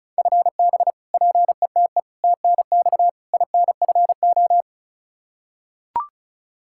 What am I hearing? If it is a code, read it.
Morse code, heard as a keyed tone, about 34 wpm, 700 Hz. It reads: FB PETE TNX INFO
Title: Practice QSO - Exchange 3 - More Info - 35wpm